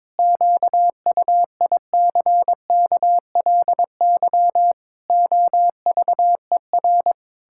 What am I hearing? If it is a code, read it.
Morse code, heard as a keyed tone, about 22 wpm, 700 Hz. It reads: QUICKLY OVER
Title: Pangram Challenge v2 - 22wpm